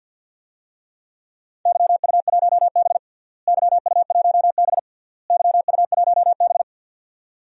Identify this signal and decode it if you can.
Morse code, heard as a keyed tone, about 47 wpm, 700 Hz. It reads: YU1B YU1B YU1B
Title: DXCC-100 Call Signs - 4 Characters - Repeated 3x - 50wpm